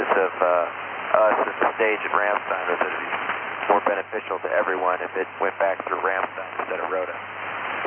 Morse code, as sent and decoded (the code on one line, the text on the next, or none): none